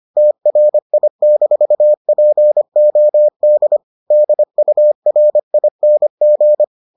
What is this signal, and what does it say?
Morse code, heard as a keyed tone, about 25 wpm, 600 Hz. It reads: TRI-POD DURING